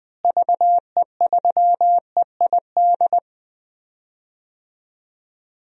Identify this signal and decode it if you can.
Morse code, heard as a keyed tone, about 20 wpm, 700 Hz. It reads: VE3EID
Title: DXCC-100 Call Signs - 6 Characters - Repeated 3x - 20wpm